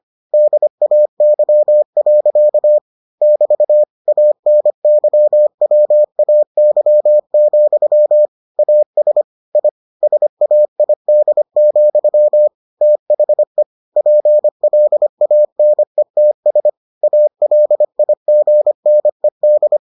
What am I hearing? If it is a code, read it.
Morse code, heard as a keyed tone, about 25 wpm, 600 Hz. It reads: DAY. <BT> ANYWAY, AS I SAID, THE PLANETS ALIGNED